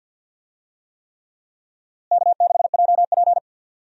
Morse code, heard as a keyed tone, about 50 wpm, 700 Hz. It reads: K6JP
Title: USA Call Signs - 4 Characters - Repeated 3x - 50wpm